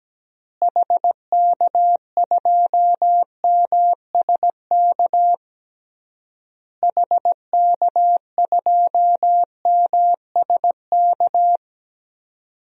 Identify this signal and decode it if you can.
Morse code, heard as a keyed tone, about 17 wpm, 700 Hz. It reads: HK2MSK HK2MSK
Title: DXCC-100 Call Signs - 6 Characters - Repeated 3x - 17wpm